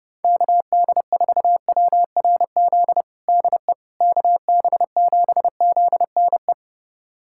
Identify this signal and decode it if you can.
Morse code, heard as a keyed tone, about 29 wpm, 700 Hz. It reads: KD4WRZ DE KB7ZNE
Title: USA Callsign DE Callsign - 6 Characters - 30wpm